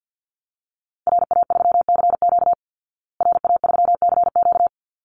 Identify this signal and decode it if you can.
Morse code, heard as a keyed tone, about 50 wpm, 700 Hz. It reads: RA3CQ RA3CQ
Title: Call Signs Repeated 3x - 50wpm